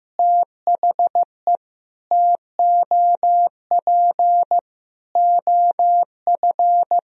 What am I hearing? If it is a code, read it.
Morse code, heard as a keyed tone, about 15 wpm, 700 Hz. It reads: THE TOP OF
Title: Sentences from Top 1000 Words - Review Part 2 - 15wpm